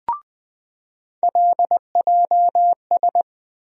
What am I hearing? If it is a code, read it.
Morse code, heard as a keyed tone, about 20 wpm, 700 Hz. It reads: LJS
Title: USA Call Sign Postfixes - 3 Letters 20wpm